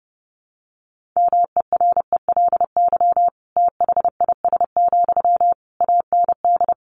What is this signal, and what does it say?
Morse code, heard as a keyed tone, about 30 wpm, 700 Hz. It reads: MERELY THIS, AND
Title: Sentences from Top 1400 Words 30wpm